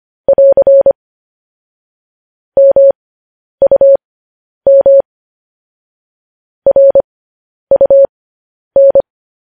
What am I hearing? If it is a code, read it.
Morse code, heard as a keyed tone, about 25 wpm, spaced as for 10 wpm, 550 Hz. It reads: <AR> MUM RUN
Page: news headlines in morse code at 10 wpm